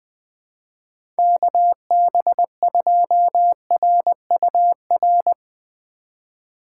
Morse code, heard as a keyed tone, about 20 wpm, 700 Hz. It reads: KB2RUR
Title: USA Call Signs - 3x Character Spacing - 20wpm